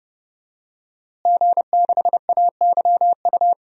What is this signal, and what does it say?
Morse code, heard as a keyed tone, about 30 wpm, 700 Hz. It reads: G6AYU
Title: DXCC-100 Call Signs - 5 Characters - Repeated 3x - 30wpm